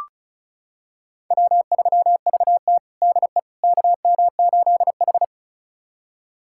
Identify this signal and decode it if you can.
Morse code, heard as a keyed tone, about 36 wpm, 700 Hz. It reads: W3VT DE KM8H
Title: USA Callsign DE Callsign - 4 Characters - 35wpm